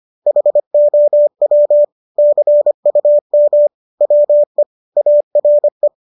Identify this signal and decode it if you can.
Morse code, heard as a keyed tone, about 25 wpm, 590 Hz. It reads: HOW CUM WE ARE